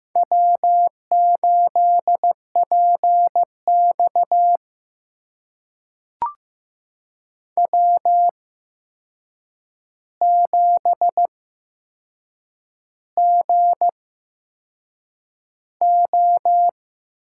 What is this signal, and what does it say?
Morse code, heard as a keyed tone, about 15 wpm, spaced as for 6 wpm, 700 Hz. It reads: W8PX W7GO
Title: USA Call Signs - 4 Characters - 8x Character Spacing - 15wpm